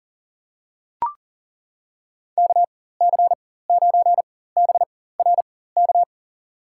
Morse code, heard as a keyed tone, about 40 wpm, spaced as for 19 wpm, 700 Hz. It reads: KC9BRK
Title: USA Call Signs - 6 Characters - 4x Character Spacing - 40wpm